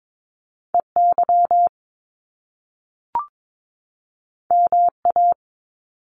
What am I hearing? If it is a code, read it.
Morse code, heard as a keyed tone, about 22 wpm, 700 Hz. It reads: EY MA
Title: Top Two Letter Combinations in Words 22wpm